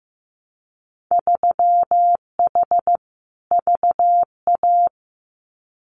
Morse code, heard as a keyed tone, about 15 wpm, 700 Hz. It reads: 3H VA
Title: Winter Field Day Exchanges Encore 15wpm